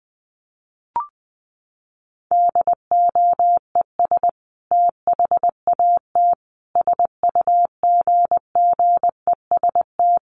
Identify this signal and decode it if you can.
Morse code, heard as a keyed tone, about 20 wpm, 700 Hz. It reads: DOES THAT SUGGEST
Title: Sentences from Top 1300 Words 20wpm